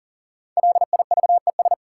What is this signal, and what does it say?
Morse code, heard as a keyed tone, about 40 wpm, 700 Hz. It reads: LIVES
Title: Sets of 5 Words Encore - New Word Drill - 40wpm